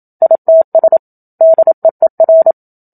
Morse code, heard as a keyed tone, about 27 wpm, 650 Hz. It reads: ITS DEER
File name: Book Two / 01-Chapter 1 - 27 WPM